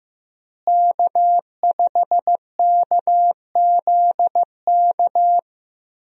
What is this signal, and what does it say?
Morse code, heard as a keyed tone, about 15 wpm, 700 Hz. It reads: K5KZK
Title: USA Call Signs - 5 Characters - Repeated 3x - 15wpm